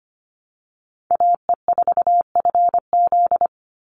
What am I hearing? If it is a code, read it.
Morse code, heard as a keyed tone, about 25 wpm, 700 Hz. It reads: AE4FZ
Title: NA SOTA Chaser Callsigns 25wpm